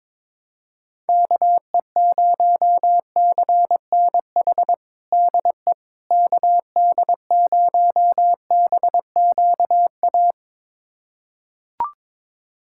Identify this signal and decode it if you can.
Morse code, heard as a keyed tone, about 22 wpm, 700 Hz. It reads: KE0CNH DE KD0BQA
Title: USA Callsign DE Callsign - 6 Characters - 22wpm